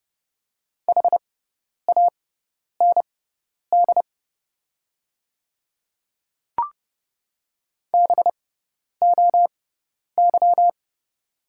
Morse code, heard as a keyed tone, about 30 wpm, spaced as for 10 wpm, 700 Hz. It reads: HAND BOY
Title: 6x Character Spacing - Top 200 Words - 30wpm